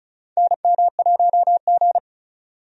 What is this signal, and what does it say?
Morse code, heard as a keyed tone, about 36 wpm, 700 Hz. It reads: NM1G
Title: USA Call Signs - 4 Characters - 35wpm